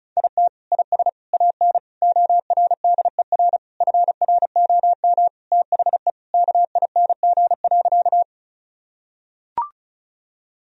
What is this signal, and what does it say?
Morse code, heard as a keyed tone, about 34 wpm, 700 Hz. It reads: IT IS AN ORDER FROM THE KING.
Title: Sentences from Top 1000 Words - Review Part 2 - 35wpm